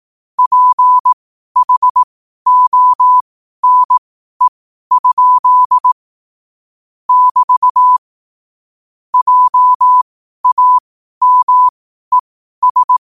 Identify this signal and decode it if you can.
Morse code, heard as a keyed tone, about 18 wpm, spaced as for 12 wpm, 1000 Hz. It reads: PHONE? <BT> JAMES
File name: qotdfiles / 12wpm